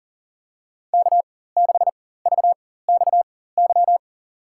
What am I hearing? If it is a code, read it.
Morse code, heard as a keyed tone, about 40 wpm, spaced as for 19 wpm, 700 Hz. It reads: K6VXY
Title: USA Call Signs - 5 Characters - 4x Character Spacing - 40wpm